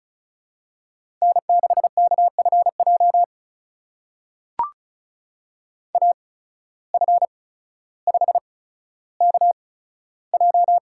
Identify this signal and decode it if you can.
Morse code, heard as a keyed tone, about 36 wpm, spaced as for 15 wpm, 700 Hz. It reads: N6KFJ AF5KJ
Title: USA Call Signs - 5 Characters - 8x Character Spacing - 35wpm